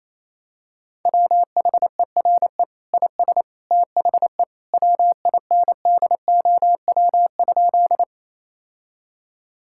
Morse code, heard as a keyed tone, about 28 wpm, 700 Hz. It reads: WHERE IS THE WINDOW?
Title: Sentences from Top 1600 Words 28wpm